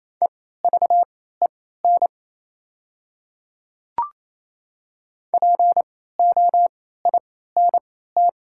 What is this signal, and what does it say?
Morse code, heard as a keyed tone, about 28 wpm, spaced as for 16 wpm, 700 Hz. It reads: EVEN POINT
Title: 3x Character Spacing - Top 300 Words - 28wpm